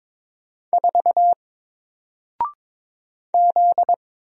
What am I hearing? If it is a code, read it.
Morse code, heard as a keyed tone, about 22 wpm, 700 Hz. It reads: 4 Z
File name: Single-Letter-Number-22wpm